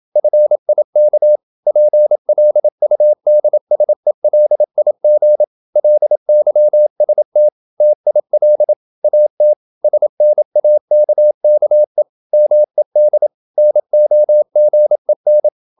Morse code, heard as a keyed tone, about 28 wpm, 600 Hz. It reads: FIK PLUDSELIG LYST TIL AT SNAKKE MED NOGEN
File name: Ferie QRQ 27wpm